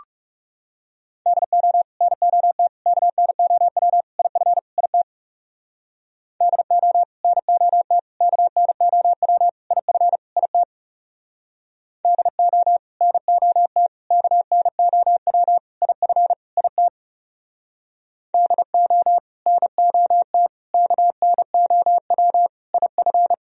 Morse code, heard as a keyed tone, about 40 wpm, 700 Hz. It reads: DO NOT KNOW IF IT DO NOT KNOW IF IT DO NOT KNOW IF IT DO NOT KNOW IF